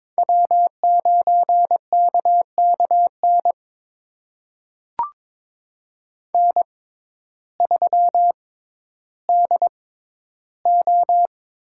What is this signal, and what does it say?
Morse code, heard as a keyed tone, about 22 wpm, spaced as for 12 wpm, 700 Hz. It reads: W9KKN N3DO
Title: USA Call Signs - 5 Characters - 6x Character Spacing - 22wpm